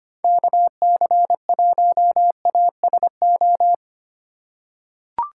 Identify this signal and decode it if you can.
Morse code, heard as a keyed tone, about 25 wpm, 700 Hz. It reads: KC1ASO
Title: USA Call Signs - 4x Character Spacing - 25wpm